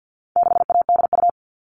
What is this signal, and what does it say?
Morse code, heard as a keyed tone, about 50 wpm, 700 Hz. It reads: 6ADU
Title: All Cut Numbers - 4-Digit 50wpm